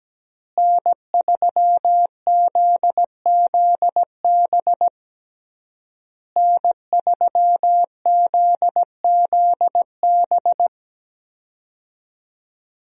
Morse code, heard as a keyed tone, about 17 wpm, 700 Hz. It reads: N3ZZB N3ZZB
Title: USA Call Signs - 5 Characters - Repeated 3x - 17wpm